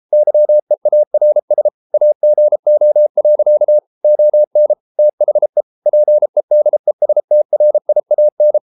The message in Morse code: YEARS AGO. ON THE PEDESTRIAN

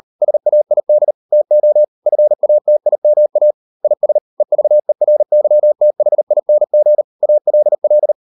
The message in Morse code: SAID TO FATIMA IS EVERYTHING ALL

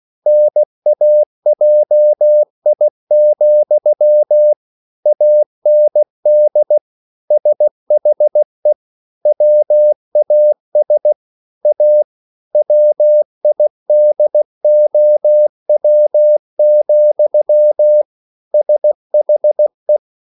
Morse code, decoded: NAJI, AND SHE WAS A WIDOW, SHE

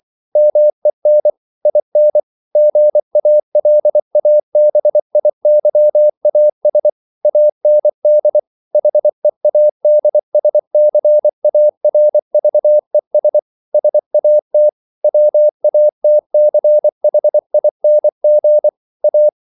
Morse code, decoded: MEN IN GALABIYAS AND HEADSCARVES SAT WATCHING A

